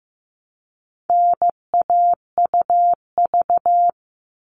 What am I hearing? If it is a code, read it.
Morse code, heard as a keyed tone, about 15 wpm, 700 Hz. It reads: NAUV
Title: All Cut Numbers - 4-Digit 15wpm